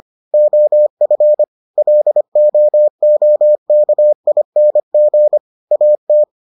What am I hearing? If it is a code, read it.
Morse code, heard as a keyed tone, about 25 wpm, 600 Hz. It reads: OF LOOKING AT